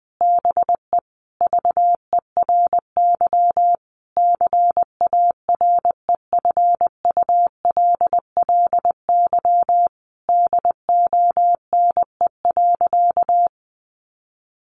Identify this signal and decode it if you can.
Morse code, heard as a keyed tone, about 20 wpm, 700 Hz. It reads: BE VERY CAREFULLY DONE.
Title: Sentences from Top 1300 Words 20wpm